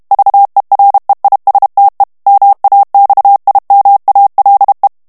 Morse code, heard as a keyed tone, about 32 wpm, 800 Hz. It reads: VEREISTE MAXIMALE
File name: HOLLANDAIS texte clair 2 vitesse 30 mots minute